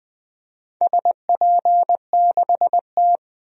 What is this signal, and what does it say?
Morse code, heard as a keyed tone, about 20 wpm, 700 Hz. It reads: SP6T
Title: DXCC-100 Call Signs - 4 Characters - 20wpm